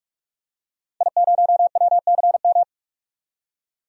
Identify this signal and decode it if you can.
Morse code, heard as a keyed tone, about 44 wpm, 700 Hz. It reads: I0WCM
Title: DXCC-100 Call Signs - 5 Characters - 45wpm